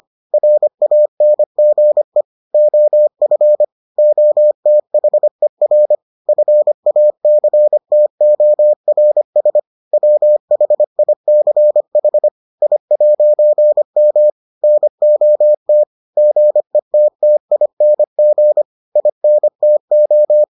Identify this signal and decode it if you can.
Morse code, heard as a keyed tone, about 25 wpm, 600 Hz. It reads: RANGE OF OTHER FACTORS WHICH I'M NOT GETTING INTO